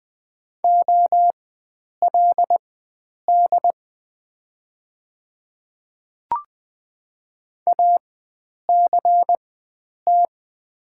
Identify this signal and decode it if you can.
Morse code, heard as a keyed tone, about 20 wpm, spaced as for 9 wpm, 700 Hz. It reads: OLD ACT
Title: 4x Character Spacing - Top 300 Words - 20wpm